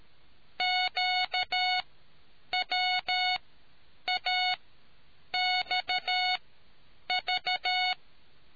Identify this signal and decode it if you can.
Morse code, heard as a keyed tone, about 13 wpm, spaced as for 8 wpm, 730 Hz. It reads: QWAXV